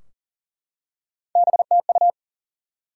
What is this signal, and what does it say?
Morse code, heard as a keyed tone, about 40 wpm, 700 Hz. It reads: BTU
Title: Top 90 QSO Elements 40wpm